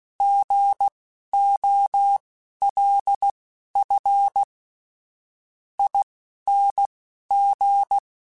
Morse code, heard as a keyed tone, about 16 wpm, spaced as for 10 wpm, 800 Hz. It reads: GOLFING
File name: Alphabetical / Alphabetical mp3s / Gs @10